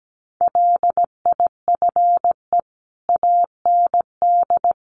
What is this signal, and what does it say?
Morse code, heard as a keyed tone, about 17 wpm, 700 Hz. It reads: LIFE AND
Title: Sentences from Top 1100 Words 17wpm